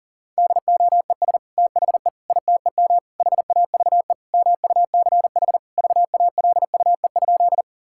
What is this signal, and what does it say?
Morse code, heard as a keyed tone, about 40 wpm, 700 Hz. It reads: DOES THE ITEM HAVE MUCH VALUE?
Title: Sentences from Top 1000 Words - Review Part 6 - 40wpm